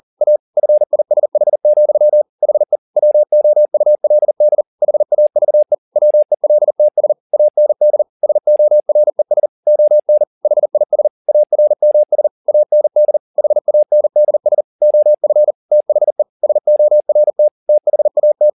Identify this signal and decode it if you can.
Morse code, heard as a keyed tone, about 40 wpm, 600 Hz. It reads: A FISH, HE WOULD HAVE WELTS AND SORES ON HIS ARMS AND HANDS OF THE SORT THAT